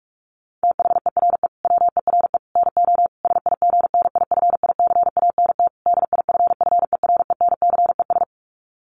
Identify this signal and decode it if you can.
Morse code, heard as a keyed tone, about 44 wpm, 700 Hz. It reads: THERE WERE NO SIGNIFICANT DIFFERENCES
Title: Sets of 5 Words Encore2 45wpm